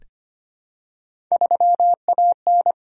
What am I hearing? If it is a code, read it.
Morse code, heard as a keyed tone, about 25 wpm, 700 Hz. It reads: 3AN
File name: Three Letter-Number Combinations - 25wpm